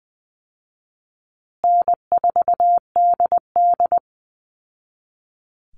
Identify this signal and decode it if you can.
Morse code, heard as a keyed tone, about 20 wpm, 700 Hz. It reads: N4DD
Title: Easy Call Signs Repeated 3x - 20wpm